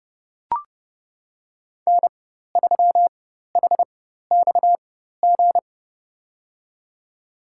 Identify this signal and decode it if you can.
Morse code, heard as a keyed tone, about 30 wpm, spaced as for 14 wpm, 700 Hz. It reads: N3HXG
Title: USA Call Signs - 5 Characters - 4x Character Spacing - 30wpm